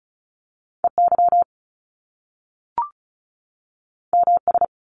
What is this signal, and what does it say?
Morse code, heard as a keyed tone, about 36 wpm, 700 Hz. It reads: EY MS
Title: Top Two Letter Combinations in Words 35wpm